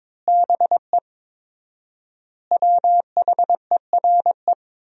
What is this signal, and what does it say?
Morse code, heard as a keyed tone, about 22 wpm, 700 Hz. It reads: BE WHERE
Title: 4x Word-Spacing - Sentences from Top 200 Words - 22wpm